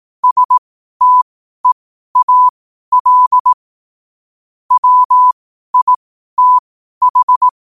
Morse code, heard as a keyed tone, about 18 wpm, spaced as for 12 wpm, 1000 Hz. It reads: STEAL WITH
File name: qotdfiles / 12wpm